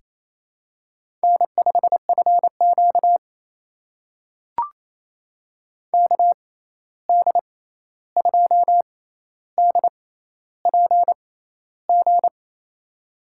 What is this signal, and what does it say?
Morse code, heard as a keyed tone, about 28 wpm, spaced as for 13 wpm, 700 Hz. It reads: N5FQ KD2DPG
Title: USA Call Signs - 6x Character Spacing - 28wpm